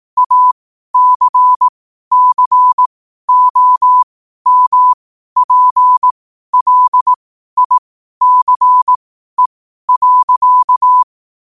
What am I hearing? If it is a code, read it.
Morse code, heard as a keyed tone, about 18 wpm, spaced as for 12 wpm, 1000 Hz. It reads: ACCOMPLICE.
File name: qotdfiles / 12wpm